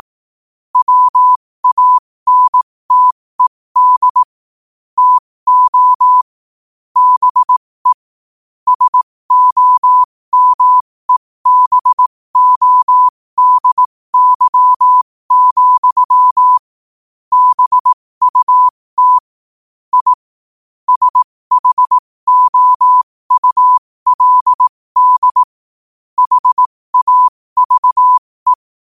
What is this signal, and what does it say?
Morse code, heard as a keyed tone, about 18 wpm, spaced as for 15 wpm, 1000 Hz. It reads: WANTED TO BE SOMEBODY, BUT I SHOULD HAVE
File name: qotdfiles / 15wpm